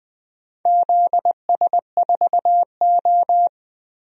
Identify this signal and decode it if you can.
Morse code, heard as a keyed tone, about 20 wpm, 700 Hz. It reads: ZS4O
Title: DXCC-100 Call Signs - 4 Characters - Repeated 3x - 20wpm